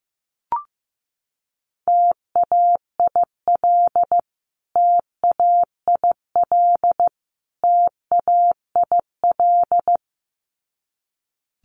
Morse code, heard as a keyed tone, about 15 wpm, 700 Hz. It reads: TAIL TAIL TAIL
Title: Top 500 Words Repeated 15wpm